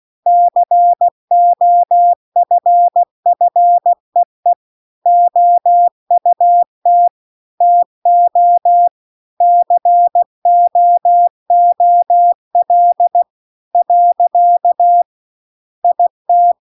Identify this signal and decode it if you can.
Morse code, heard as a keyed tone, about 16 wpm, 700 Hz. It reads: COFFEE OUT TO COOL . IT